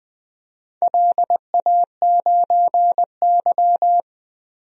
Morse code, heard as a keyed tone, about 20 wpm, 700 Hz. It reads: LA9Y
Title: DXCC-100 Call Signs - 4 Characters - Repeated 2x - 20wpm